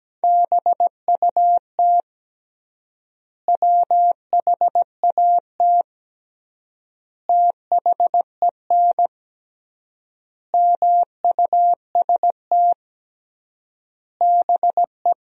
Morse code, heard as a keyed tone, about 17 wpm, 700 Hz. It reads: BUT WHAT THEN MUST BE
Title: 3x Word-Spacing - Sentences from Top 500 Words - 17wpm